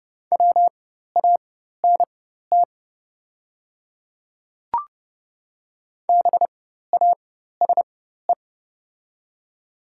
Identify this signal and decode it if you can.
Morse code, heard as a keyed tone, about 30 wpm, spaced as for 14 wpm, 700 Hz. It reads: WANT BASE
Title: 4x Character Spacing - Top 300 Words - 30wpm